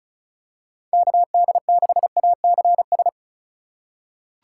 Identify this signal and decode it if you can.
Morse code, heard as a keyed tone, about 36 wpm, 700 Hz. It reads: KD6ACS
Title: USA Call Signs - 6 Characters - 2x Character Spacing - 35wpm